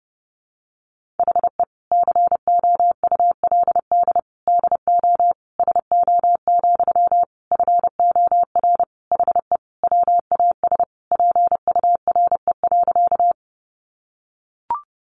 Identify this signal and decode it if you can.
Morse code, heard as a keyed tone, about 30 wpm, 700 Hz. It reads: HE COULD DO SO, FOR HE WAS PURE.
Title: Sentences from Top 1500 Words 30wpm